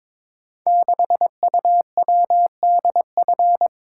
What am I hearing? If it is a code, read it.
Morse code, heard as a keyed tone, about 22 wpm, 690 Hz. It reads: 6UWDF
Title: Code Groups - 5 Letter:Numbers - 22wpm